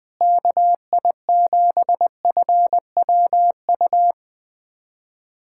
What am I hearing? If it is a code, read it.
Morse code, heard as a keyed tone, about 20 wpm, 700 Hz. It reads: KI7FWU
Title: USA Callsign DE Callsign - 6 Characters - 20wpm